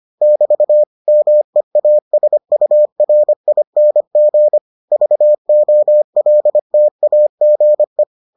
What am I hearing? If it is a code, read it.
Morse code, heard as a keyed tone, about 25 wpm, 600 Hz. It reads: <BT> MEASURING VOLTAGE